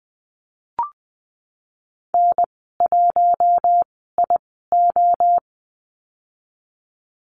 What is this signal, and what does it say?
Morse code, heard as a keyed tone, about 20 wpm, spaced as for 15 wpm, 700 Hz. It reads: N1IO
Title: 055 Lesson 10g - Letter r - Call Signs 20wpm